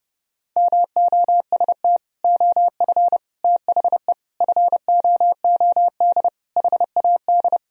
Morse code, heard as a keyed tone, about 30 wpm, 700 Hz. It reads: MOST OF THE FOOD HAD
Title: Sentences from Top 1000 Words - Review Part 1 - 30wpm